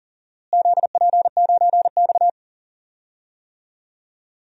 Morse code, heard as a keyed tone, about 40 wpm, 700 Hz. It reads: ZP9X
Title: DXCC-100 Call Signs - 4 Characters - Repeated 2x - 40wpm